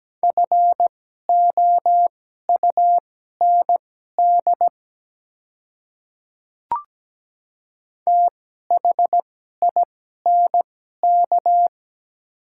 Morse code, heard as a keyed tone, about 17 wpm, spaced as for 12 wpm, 700 Hz. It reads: FOUND THINK
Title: 2x Character Spacing - Top 300 Words - 17wpm